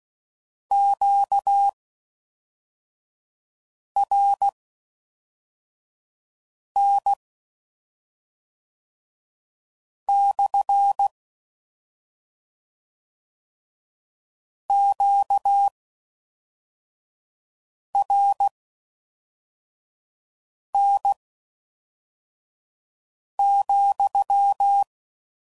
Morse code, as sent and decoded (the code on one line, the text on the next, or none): --.- .-. -. -..-. / --.- .-. -. --..--
QRN/ QRN,